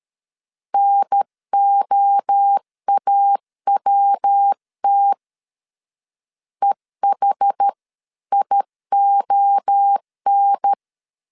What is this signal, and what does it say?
Morse code, heard as a keyed tone, about 13 wpm, 800 Hz. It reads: NOAWT EHION
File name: CW / DET / 1 3 1500C S A 12wpm 2905